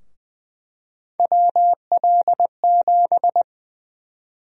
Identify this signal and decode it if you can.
Morse code, heard as a keyed tone, about 20 wpm, 700 Hz. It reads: WL7
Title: USA Call Sign Prefixes - Repeated 2x - 20wpm